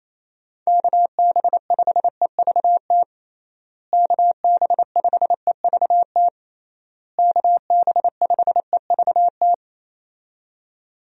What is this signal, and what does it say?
Morse code, heard as a keyed tone, about 28 wpm, 700 Hz. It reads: KB5EVT KB5EVT KB5EVT
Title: USA Call Signs - 6 Characters - Repeated 3x - 28wpm